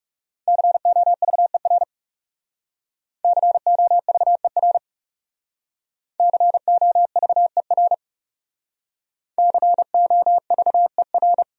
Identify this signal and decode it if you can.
Morse code, heard as a keyed tone, about 37 wpm, 700 Hz. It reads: COVER COVER COVER COVER
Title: Speed-Racing Top 500 Words 30wpm